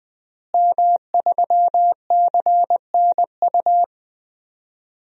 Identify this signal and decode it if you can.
Morse code, heard as a keyed tone, about 20 wpm, 700 Hz. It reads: M3CNU
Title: DXCC-100 Call Signs - 5 Characters - Repeated 3x - 20wpm